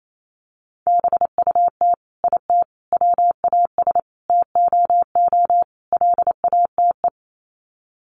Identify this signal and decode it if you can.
Morse code, heard as a keyed tone, about 28 wpm, 700 Hz. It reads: BUT IT WAS TOO LATE
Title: Sets of 5 Words Encore2 28wpm